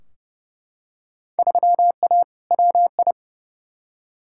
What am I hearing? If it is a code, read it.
Morse code, heard as a keyed tone, about 31 wpm, 700 Hz. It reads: 3A WI
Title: Field Day Exchanges - 30wpm